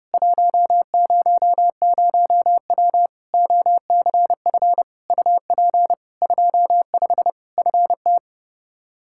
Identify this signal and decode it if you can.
Morse code, heard as a keyed tone, about 32 wpm, 700 Hz. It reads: 100W OCF UP 25 FT
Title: Practice QSO - Exchange 2 - Info - 30wpm